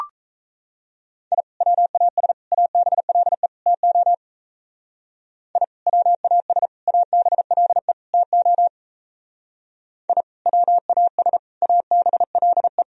Morse code, heard as a keyed tone, about 37 wpm, 700 Hz. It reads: I WAS ABLE TO I WAS ABLE TO I WAS ABLE